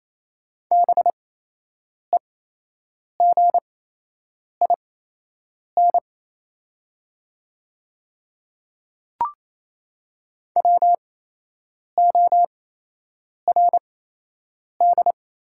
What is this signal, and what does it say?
Morse code, heard as a keyed tone, about 28 wpm, spaced as for 8 wpm, 700 Hz. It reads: BEGIN WORD
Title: 8x Character Spacing - Top 500 Words - 28wpm